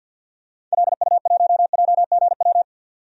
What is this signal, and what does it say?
Morse code, heard as a keyed tone, about 53 wpm, 700 Hz. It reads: LU1JGW
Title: DXCC-100 Call Signs - 6 Characters - 50wpm